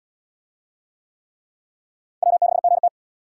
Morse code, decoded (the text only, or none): W6GT